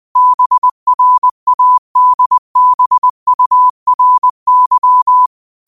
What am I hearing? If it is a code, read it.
Morse code, heard as a keyed tone, about 20 wpm, 1000 Hz. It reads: BRADBURY